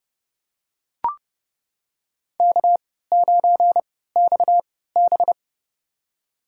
Morse code, heard as a keyed tone, about 30 wpm, spaced as for 17 wpm, 700 Hz. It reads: K9XB
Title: USA Call Signs - 4 Characters - 3x Character Spacing - 30wpm